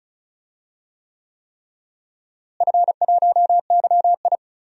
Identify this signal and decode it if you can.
Morse code, heard as a keyed tone, about 34 wpm, 700 Hz. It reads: F1YI
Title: DXCC-100 Call Signs - 4 Characters - Repeated 3x - 35wpm